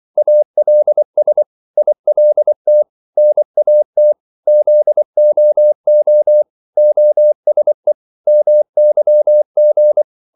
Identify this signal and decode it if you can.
Morse code, heard as a keyed tone, about 24 wpm, 600 Hz. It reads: ALS ILT NAT ZOO OSE MYG
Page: Korte danske ord 24wpm